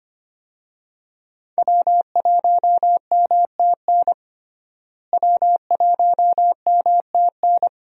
Morse code, heard as a keyed tone, about 25 wpm, 700 Hz. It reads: W1MTN W1MTN
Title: USA Call Signs - 5 Characters - Repeated 3x - 25wpm